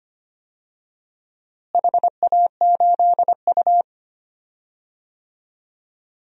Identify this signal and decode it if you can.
Morse code, heard as a keyed tone, about 25 wpm, 700 Hz. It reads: HA8U
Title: DXCC-100 Call Signs - 4 Characters - Repeated 3x - 25wpm